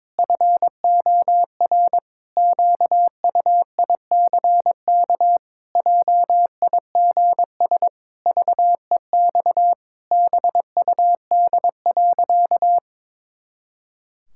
Morse code, heard as a keyed tone, about 22 wpm, 700 Hz. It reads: FOR QUICK JIGS VEX BUD.
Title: Pangram Challenge v2 - 22wpm